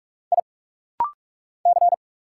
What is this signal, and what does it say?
Morse code, heard as a keyed tone, about 46 wpm, 700 Hz. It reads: I C
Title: Single Letter-Number - Rapid-Fire 45wpm